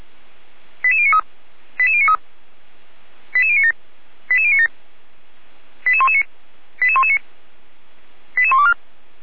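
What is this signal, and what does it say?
Five-tone selective call (ZVEI), heard as quick runs of five tones, tones of about 70 ms. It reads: 88093 88093 88097 88097 88119 88119 88124
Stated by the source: ZVEI five-tone call received by SDR.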